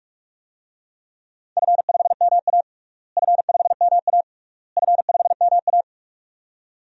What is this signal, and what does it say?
Morse code, heard as a keyed tone, about 46 wpm, 700 Hz. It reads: F5MA F5MA F5MA
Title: DXCC-100 Call Signs - 4 Characters - Repeated 3x - 45wpm